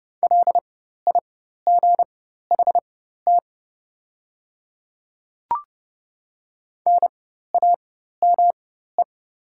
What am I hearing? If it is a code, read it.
Morse code, heard as a keyed tone, about 30 wpm, spaced as for 14 wpm, 690 Hz. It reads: LIGHT NAME
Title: 4x Character Spacing - Top 200 Words - 30wpm